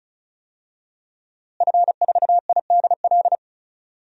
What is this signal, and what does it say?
Morse code, heard as a keyed tone, about 34 wpm, 700 Hz. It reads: F4IDL
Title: DXCC-100 Call Signs - 5 Characters - Repeated 2x - 35wpm